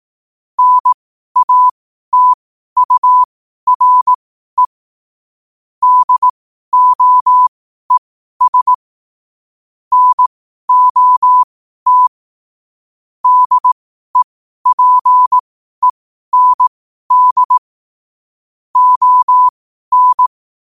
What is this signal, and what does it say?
Morse code, heard as a keyed tone, about 18 wpm, spaced as for 12 wpm, 1000 Hz. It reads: NATURE DOES NOT DEPEND ON
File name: qotdfiles / 12wpm